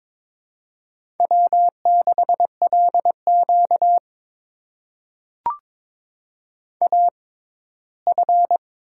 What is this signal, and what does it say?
Morse code, heard as a keyed tone, about 22 wpm, spaced as for 15 wpm, 700 Hz. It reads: W6LQ AF
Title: USA Call Signs - 4 Characters - 6x Character Spacing - 22wpm